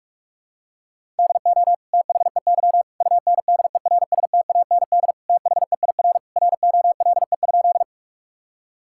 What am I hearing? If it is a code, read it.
Morse code, heard as a keyed tone, about 46 wpm, 700 Hz. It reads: DO THEY UNDERSTAND THEIR ROLE?
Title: Sentences from Top 1000 Words - Review Part 6 - 45wpm